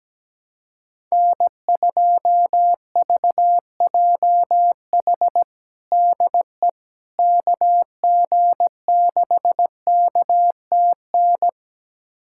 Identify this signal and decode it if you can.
Morse code, heard as a keyed tone, about 17 wpm, 700 Hz. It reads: N2VJH DE KG6KTN
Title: USA Callsign DE Callsign 17wpm